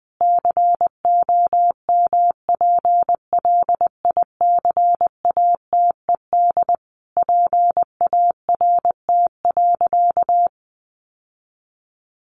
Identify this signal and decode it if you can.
Morse code, heard as a keyed tone, about 20 wpm, 700 Hz. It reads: COMPLICATED PART.
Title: Sentences from Top 1200 Words 20wpm